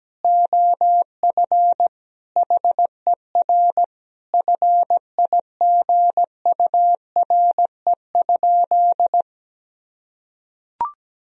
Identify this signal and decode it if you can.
Morse code, heard as a keyed tone, about 17 wpm, 700 Hz. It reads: OF HER FIGURE?
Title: Sentences from Top 1000 Words - Review Part 2 - 17wpm